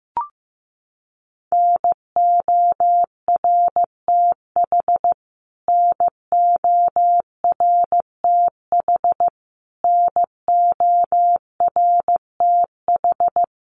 Morse code, heard as a keyed tone, about 15 wpm, 700 Hz. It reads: NORTH NORTH NORTH
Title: Top 500 Words Repeated 15wpm